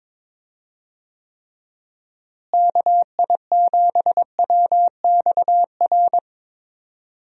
Morse code, heard as a keyed tone, about 21 wpm, 700 Hz. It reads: KI7WXR
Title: USA Call Signs - 6 Characters - Repeated 3x - 22wpm